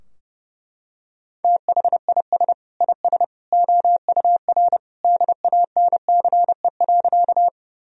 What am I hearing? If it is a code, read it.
Morse code, heard as a keyed tone, about 32 wpm, 700 Hz. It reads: THIS IS OUR DANCE.
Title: Sentences from Top 1000 Words - Review Part 4 - 30wpm